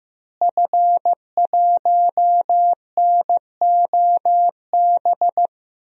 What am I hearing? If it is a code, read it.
Morse code, heard as a keyed tone, about 15 wpm, 700 Hz. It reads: F1NOB
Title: DXCC-100 Call Signs - 5 Characters - Repeated 3x - 15wpm